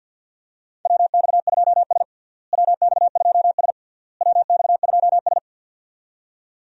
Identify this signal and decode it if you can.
Morse code, heard as a keyed tone, about 46 wpm, 700 Hz. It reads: WX2S WX2S WX2S
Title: USA Call Signs - 4 Characters - Repeated 3x - 50wpm